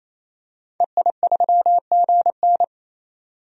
Morse code, heard as a keyed tone, about 28 wpm, 700 Hz. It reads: EI3GN
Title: DXCC-100 Call Signs - 5 Characters - 28wpm